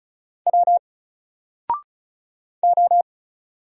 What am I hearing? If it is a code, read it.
Morse code, heard as a keyed tone, about 35 wpm, 700 Hz. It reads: W O
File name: Single-Letter-Number-35wpm